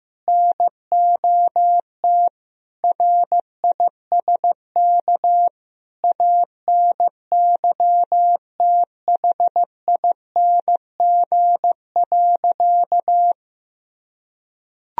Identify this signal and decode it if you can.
Morse code, heard as a keyed tone, about 15 wpm, 700 Hz. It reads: NOT RISK ANYTHING.
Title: Sentences from Top 1000 Words - Review Part 4 - 15wpm